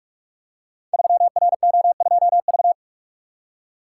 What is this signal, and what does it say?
Morse code, heard as a keyed tone, about 44 wpm, 700 Hz. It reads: 3RO2V
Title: Code Groups - 5 Letter:Numbers - 45wpm